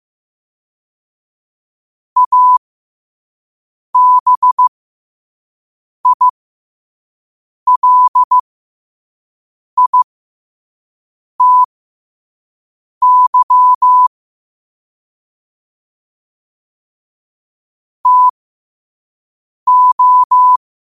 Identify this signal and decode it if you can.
Morse code, heard as a keyed tone, about 15 wpm, spaced as for 5 wpm, 1000 Hz. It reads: ABILITY TO